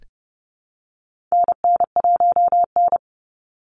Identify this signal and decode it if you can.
Morse code, heard as a keyed tone, about 30 wpm, 700 Hz. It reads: NN1N
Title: Call Signs Repeated 3x - 30wpm